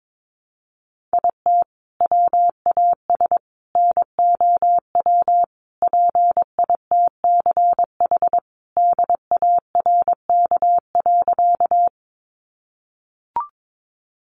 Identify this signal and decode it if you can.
Morse code, heard as a keyed tone, about 22 wpm, 700 Hz. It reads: IT WAS NOW PITCH DARK.
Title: Sentences from Top 1300 Words 22wpm